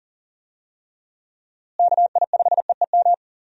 Morse code, heard as a keyed tone, about 40 wpm, 700 Hz. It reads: KI5EEM
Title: USA Call Signs - 6 Characters - Repeated 2x - 40wpm